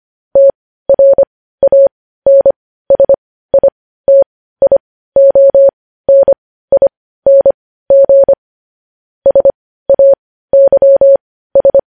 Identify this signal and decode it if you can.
Morse code, heard as a keyed tone, about 25 wpm, spaced as for 15 wpm, 550 Hz. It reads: TRANSITIONING SAYS
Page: news headlines in morse code at 15 wpm